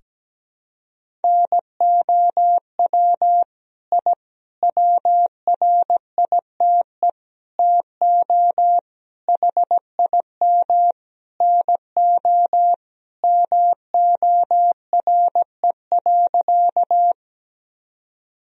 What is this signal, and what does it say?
Morse code, heard as a keyed tone, about 17 wpm, 700 Hz. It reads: NOW I WRITE TO HIM NO MORE.
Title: Sentences from Top 1000 Words - Review Part 1 - 17wpm